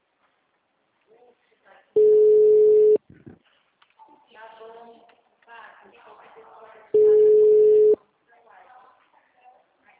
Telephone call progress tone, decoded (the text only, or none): ringback tone